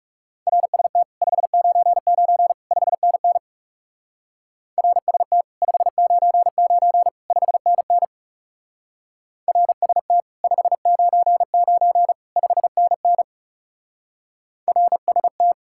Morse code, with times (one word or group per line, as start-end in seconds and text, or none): 0.5-1.0 RST
1.2-2.5 599
2.7-3.4 5NN
4.8-5.4 RST
5.6-7.1 599
7.3-8.0 5NN
9.5-10.2 RST
10.4-12.1 599
12.4-13.2 5NN
14.7-15.5 RST